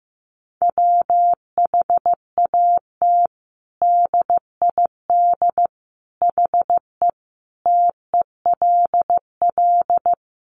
Morse code, heard as a keyed tone, about 15 wpm, 700 Hz. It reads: WHAT DID HE TELL
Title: Sentences from Top 200 Words 15wpm